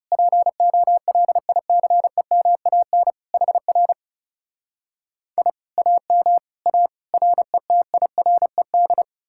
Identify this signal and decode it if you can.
Morse code, heard as a keyed tone, about 38 wpm, 700 Hz. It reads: POLICEMAN HR I AM A RETIRED
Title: Speed-Racing Common QSO Phrases Encore - 30wpm